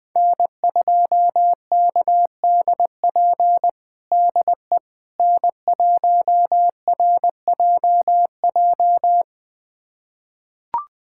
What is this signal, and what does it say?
Morse code, heard as a keyed tone, about 20 wpm, 700 Hz. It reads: N2KDP DE N1RJJ
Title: USA Callsign DE Callsign - 5 Characters - 20wpm